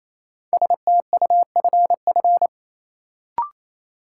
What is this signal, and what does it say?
Morse code, heard as a keyed tone, about 28 wpm, 700 Hz. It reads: STUFF
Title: Sentences from Top 1400 Words - New Word Drill - 28wpm